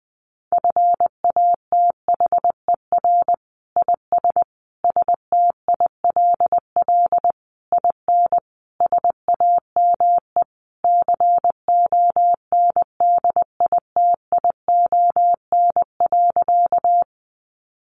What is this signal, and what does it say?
Morse code, heard as a keyed tone, about 19 wpm, 700 Hz. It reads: FATHER IS STILL IN SAME CONDITION.
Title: Sentences from Top 1000 Words 20wpm